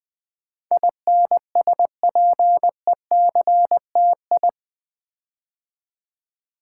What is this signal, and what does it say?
Morse code, heard as a keyed tone, about 20 wpm, 700 Hz. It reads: INSPECTI
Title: Build A Word - 4 Letters At A Time - 10-Letter Words - 20wpm